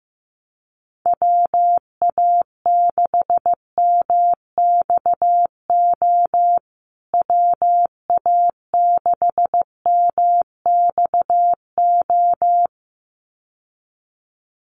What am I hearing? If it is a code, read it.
Morse code, heard as a keyed tone, about 15 wpm, 700 Hz. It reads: WA6MXO WA6MXO
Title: USA Call Signs Repeated 2x 15wpm